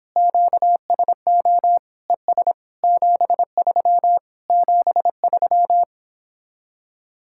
Morse code, heard as a keyed tone, about 26 wpm, 700 Hz. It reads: QSO ES 73 73